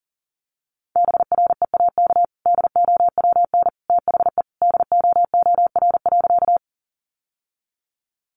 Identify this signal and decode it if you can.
Morse code, heard as a keyed tone, about 40 wpm, 700 Hz. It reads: BREAK DOWN THE DOOR.
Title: Sentences from Top 900 Words 40wpm